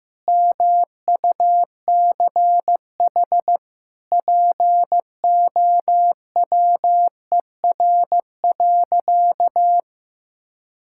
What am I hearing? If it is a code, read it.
Morse code, heard as a keyed tone, about 14 wpm, 700 Hz. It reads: MUCH POWER.
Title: Sentences from Top 1000 Words - Review Part 5 - 15wpm